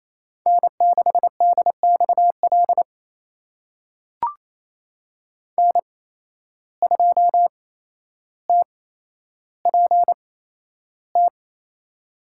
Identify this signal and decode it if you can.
Morse code, heard as a keyed tone, about 27 wpm, spaced as for 12 wpm, 700 Hz. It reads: N6DXL N2TPT
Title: USA Call Signs - 5 Characters - 8x Character Spacing - 28wpm